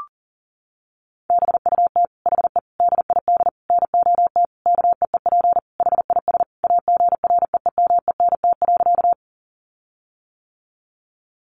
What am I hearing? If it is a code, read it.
Morse code, heard as a keyed tone, about 40 wpm, 700 Hz. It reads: BUT HE DID NOT KEEP HIS AGREEMENT.
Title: Sentences from Top 1500 Words 40wpm